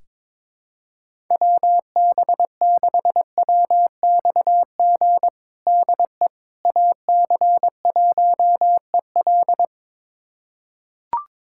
Morse code, heard as a keyed tone, about 22 wpm, 700 Hz. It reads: WB6WXG DE AC1EL
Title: USA Callsign DE Callsign 22wpm